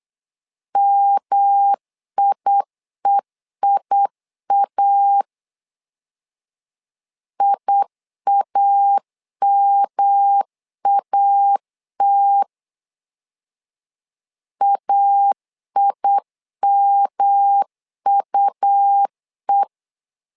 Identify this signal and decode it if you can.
Morse code, heard as a keyed tone, about 8 wpm, 800 Hz. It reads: MIEIA IAMAT AIMUE
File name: LEZIONI / 3 MWRO / 3 M 8wpm 2300